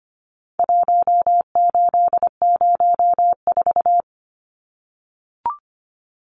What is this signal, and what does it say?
Morse code, heard as a keyed tone, about 25 wpm, 700 Hz. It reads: 1804
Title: Numbers - Contest Style - 4-Digit 25wpm